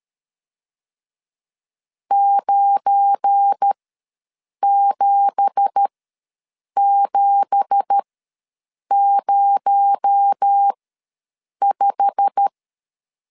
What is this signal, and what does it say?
Morse code, heard as a keyed tone, about 13 wpm, spaced as for 7 wpm, 800 Hz. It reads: 97705
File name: LEZIONI / 9 2867 ws 12wpm 3431